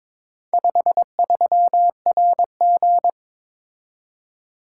Morse code, heard as a keyed tone, about 22 wpm, 700 Hz. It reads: 53RG
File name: practice / 22wpm / Four Letter-Number Combinations - 22wpm